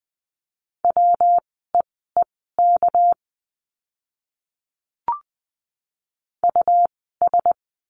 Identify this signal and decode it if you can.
Morse code, heard as a keyed tone, about 20 wpm, spaced as for 15 wpm, 700 Hz. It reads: WEEK US
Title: 206 Lesson 33f - Letter j - Words 20wpm